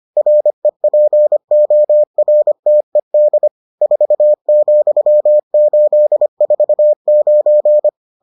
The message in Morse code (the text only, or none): REPORTED 4,849